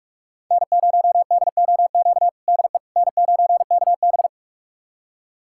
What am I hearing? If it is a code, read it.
Morse code, heard as a keyed tone, about 46 wpm, 700 Hz. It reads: N0DOQ DE N9KB
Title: USA Callsign DE Callsign 45wpm